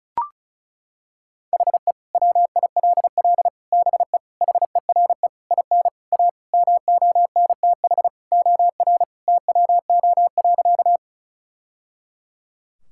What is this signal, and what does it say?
Morse code, heard as a keyed tone, about 34 wpm, 700 Hz. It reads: HE WILL BE HERE IN A MONTH OR TWO.
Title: Sentences from Top 1000 Words - Review Part 5 - 35wpm